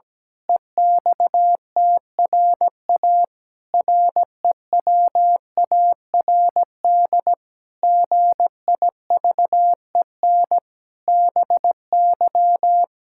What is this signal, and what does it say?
Morse code, heard as a keyed tone, about 17 wpm, 700 Hz. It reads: EXTRA REWARD GIVEN BY
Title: Pangram Challenge v2 - 17wpm